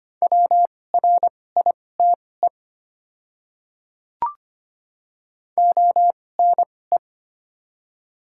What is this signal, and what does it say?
Morse code, heard as a keyed tone, about 25 wpm, spaced as for 18 wpm, 700 Hz. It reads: WRITE ONE
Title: 2x Character Spacing - Top 100 Words - 25wpm